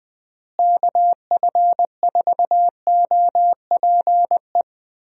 Morse code, heard as a keyed tone, about 21 wpm, 700 Hz. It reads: KF4OPE
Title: USA Call Signs - 6 Characters - Repeated 2x - 20wpm